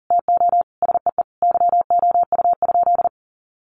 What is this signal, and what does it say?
Morse code, heard as a keyed tone, about 40 wpm, 700 Hz. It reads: TO SEE YOU?
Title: Sentences from Top 1000 Words 40wpm